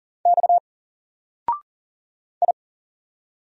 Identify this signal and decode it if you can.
Morse code, heard as a keyed tone, about 40 wpm, 700 Hz. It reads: X I
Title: Single Character (Letter:Number:Punctuation:Prosign) 40wpm